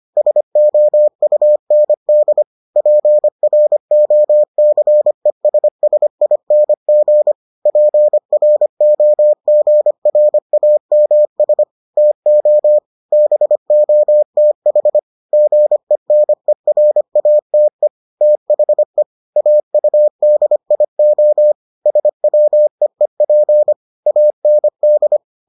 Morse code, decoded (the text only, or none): SOUND PROCESSING PROGRAMS TO BOTH GENERATE THE AUDIO SWEEP AND